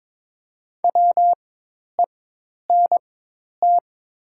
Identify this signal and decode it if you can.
Morse code, heard as a keyed tone, about 22 wpm, spaced as for 10 wpm, 700 Hz. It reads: WENT
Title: 4x Character Spacing - Top 200 Words - 22wpm